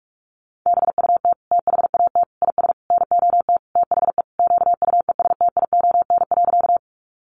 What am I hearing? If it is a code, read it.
Morse code, heard as a keyed tone, about 44 wpm, 700 Hz. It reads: BUT THAT IS NOT THE QUESTION.
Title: Sentences from Top 600 Words 45wpm